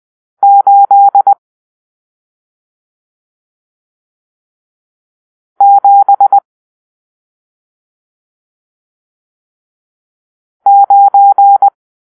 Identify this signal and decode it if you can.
Morse code, heard as a keyed tone, about 20 wpm, 800 Hz. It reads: 8 7 9
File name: arrl / Grp-7 Drill-1